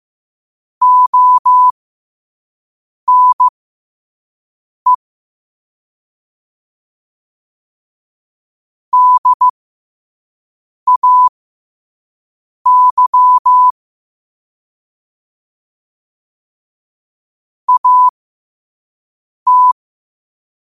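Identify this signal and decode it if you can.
Morse code, heard as a keyed tone, about 15 wpm, spaced as for 5 wpm, 1000 Hz. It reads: ONE DAY AT
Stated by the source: Morse code quote of the day sent at 5 WPM.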